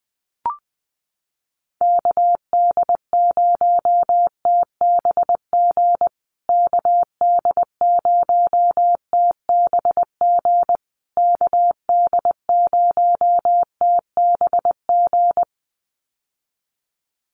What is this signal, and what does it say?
Morse code, heard as a keyed tone, about 19 wpm, 700 Hz. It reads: KD0TBG KD0TBG KD0TBG
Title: USA Call Signs Repeated 3x 20wpm